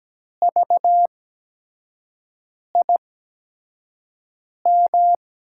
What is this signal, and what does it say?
Morse code, heard as a keyed tone, about 17 wpm, spaced as for 5 wpm, 700 Hz. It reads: VIM
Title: USA Call Signs - 5 Characters - 8x Character Spacing - 17wpm